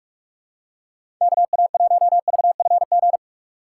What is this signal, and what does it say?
Morse code, heard as a keyed tone, about 44 wpm, 700 Hz. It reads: KA1VFG
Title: USA Call Signs - 6 Characters - 2x Character Spacing - 45wpm